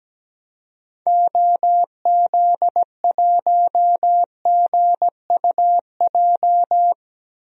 Morse code, heard as a keyed tone, about 17 wpm, 700 Hz. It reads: OZ1GUJ
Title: DXCC-100 Call Signs - 6 Characters - Repeated 3x - 17wpm